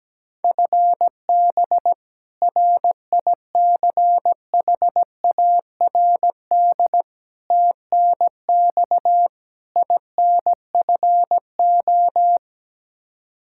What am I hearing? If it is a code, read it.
Morse code, heard as a keyed tone, about 17 wpm, 700 Hz. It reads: FB RICHARD TNX INFO
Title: Practice QSO - Exchange 3 - More Info - 17wpm